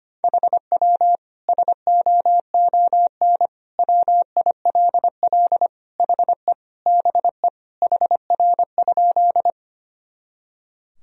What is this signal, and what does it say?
Morse code, heard as a keyed tone, about 25 wpm, 700 Hz. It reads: HW SOON WILL HE BE HR?
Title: Sentences from Top 400 Words and Abbreviations 25wpm